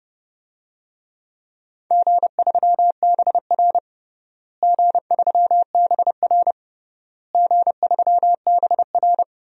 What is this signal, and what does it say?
Morse code, heard as a keyed tone, about 31 wpm, 700 Hz. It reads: G3BR G3BR G3BR
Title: DXCC-100 Call Signs - 4 Characters - Repeated 3x - 30wpm